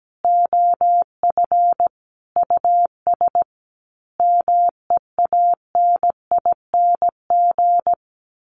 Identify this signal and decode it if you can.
Morse code, heard as a keyed tone, about 17 wpm, 700 Hz. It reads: OF US MEANING
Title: A Sound Of Thunder 17wpm